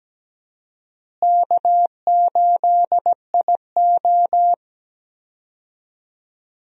Morse code, heard as a keyed tone, about 17 wpm, 700 Hz. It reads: K8IO
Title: USA Call Signs - 4 Characters - Repeated 2x - 17wpm